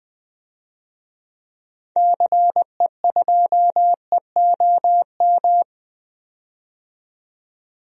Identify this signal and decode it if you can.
Morse code, heard as a keyed tone, about 21 wpm, 700 Hz. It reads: CE2EOM
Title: DXCC-100 Call Signs - 6 Characters - 20wpm